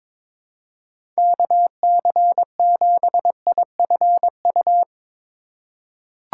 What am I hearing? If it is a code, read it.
Morse code, heard as a keyed tone, about 22 wpm, 700 Hz. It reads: KC7IFU
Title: USA Call Signs - 6 Characters - 4x Character Spacing - 22wpm